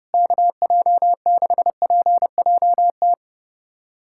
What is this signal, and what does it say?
Morse code, heard as a keyed tone, about 32 wpm, 700 Hz. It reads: KJ6PJT
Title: USA Callsign DE Callsign - 6 Characters - 30wpm